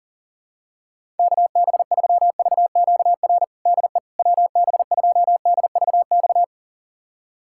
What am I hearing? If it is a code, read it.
Morse code, heard as a keyed tone, about 40 wpm, 700 Hz. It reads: KB3VQR DE WB2DVX
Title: USA Callsign DE Callsign - 6 Characters - 40wpm